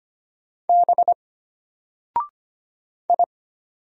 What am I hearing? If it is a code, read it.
Morse code, heard as a keyed tone, about 25 wpm, 700 Hz. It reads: B I
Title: Single Character (Letter:Number:Punctuation:Prosign) Alt 25wpm